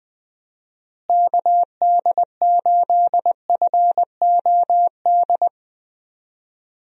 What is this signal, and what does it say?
Morse code, heard as a keyed tone, about 20 wpm, 700 Hz. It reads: KD8FOD
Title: USA Call Signs - 6 Characters - Repeated 2x - 20wpm